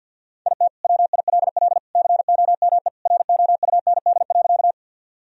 Morse code, heard as a keyed tone, about 45 wpm, 700 Hz. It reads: IT WILL COME ROUND.
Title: Sentences from Top 1000 Words - Review Part 1 - 50wpm